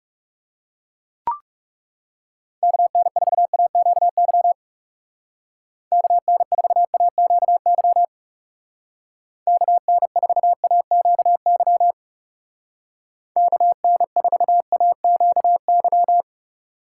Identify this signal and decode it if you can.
Morse code, heard as a keyed tone, about 38 wpm, 700 Hz. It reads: KN4AQY KN4AQY KN4AQY KN4AQY